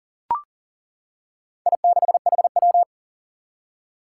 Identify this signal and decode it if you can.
Morse code, heard as a keyed tone, about 40 wpm, 700 Hz. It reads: I6HW
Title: Four Challenging Letter-Number Combinations 40wpm